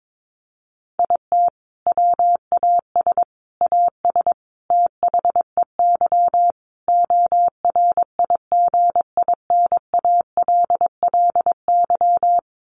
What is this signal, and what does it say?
Morse code, heard as a keyed tone, about 22 wpm, 700 Hz. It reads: IT WAS AS THEY ORIGINALLY
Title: Sentences from Top 1500 Words 22wpm